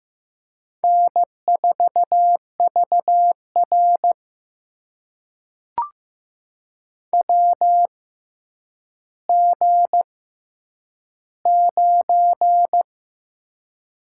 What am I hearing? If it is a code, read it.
Morse code, heard as a keyed tone, about 15 wpm, spaced as for 9 wpm, 700 Hz. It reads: N4VR WG9
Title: USA Call Signs - 4 Characters - 6x Character Spacing - 15wpm